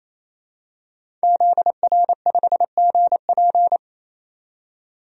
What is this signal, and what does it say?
Morse code, heard as a keyed tone, about 27 wpm, 700 Hz. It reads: ZR5GP
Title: DXCC-100 Call Signs - 5 Characters - 28wpm